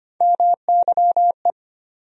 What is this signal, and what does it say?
Morse code, heard as a keyed tone, about 25 wpm, 700 Hz. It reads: MYE
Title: USA Call Sign Postfixes - 3 Letters 25wpm